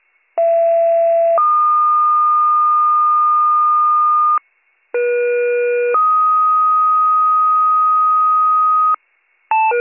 Two-tone page, heard as a one-second tone, then a three-second tone, about 670 then 1220 Hz, about 480 then 1250 Hz. It is heard from 0.4 to 9.0 s.